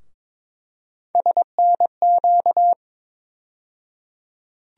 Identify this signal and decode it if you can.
Morse code, heard as a keyed tone, about 23 wpm, 700 Hz. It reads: SNQ
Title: USA Call Sign Postfixes - 3 Letters 22wpm